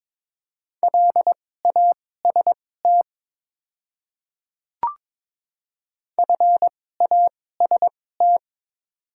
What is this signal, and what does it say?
Morse code, heard as a keyed tone, about 22 wpm, spaced as for 16 wpm, 700 Hz. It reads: LAST FAST
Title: 2x Character Spacing - Top 500 Words - 22wpm